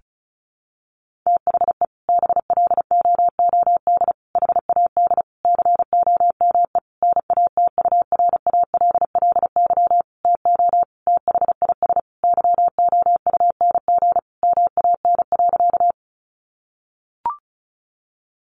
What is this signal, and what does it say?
Morse code, heard as a keyed tone, about 34 wpm, 700 Hz. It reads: THE BLOOD HAD COME NATURALLY TO THIS YOUNG MAN.
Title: Sentences from Top 1300 Words 35wpm